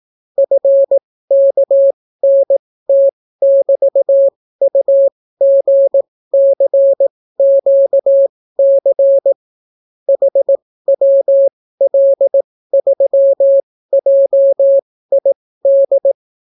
Morse code, heard as a keyed tone, about 18 wpm, spaced as for 14 wpm, 550 Hz. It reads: FKNT<BT>UGCQC HWL3JID